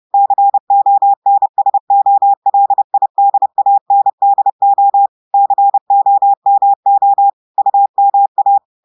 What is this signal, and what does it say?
Morse code, heard as a keyed tone, about 30 wpm, 800 Hz. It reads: CONSOLIDANDO COMO UMA
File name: audios / portugais / portugais texte en clair 1 vitesse 30 mots minutes